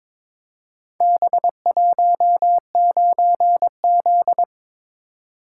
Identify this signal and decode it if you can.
Morse code, heard as a keyed tone, about 22 wpm, 700 Hz. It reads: B19Z
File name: practice / 22wpm / Four Letter-Number Combinations - 22wpm